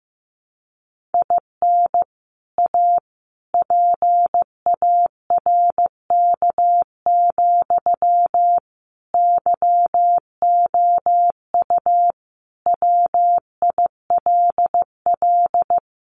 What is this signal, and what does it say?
Morse code, heard as a keyed tone, about 14 wpm, 700 Hz. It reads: IN A PARK, YOU WILL
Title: Sentences from Top 1300 Words 15wpm